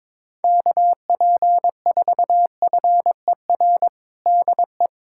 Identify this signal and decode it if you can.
Morse code, heard as a keyed tone, about 22 wpm, 700 Hz. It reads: KP4FER DE
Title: USA Callsign DE Callsign - 6 Characters - 22wpm